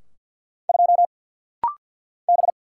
Morse code, heard as a keyed tone, about 50 wpm, 700 Hz. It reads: 2 B
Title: Single Letter-Number - Rapid-Fire 50wpm